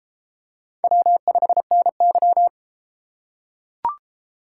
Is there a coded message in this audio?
Yes: Morse code from 0.8 to 2.5 s, heard as a keyed tone, about 29 wpm, 700 Hz. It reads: W5NY